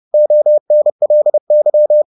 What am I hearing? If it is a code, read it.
Morse code, heard as a keyed tone, about 30 wpm, 600 Hz. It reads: ONLY